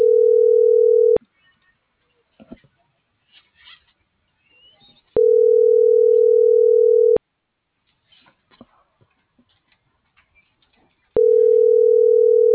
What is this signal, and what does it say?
Ringback tone (North America), 440 + 480 Hz.
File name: CENTRAL AGUILERA / out ram.lmorena.2025 - .19391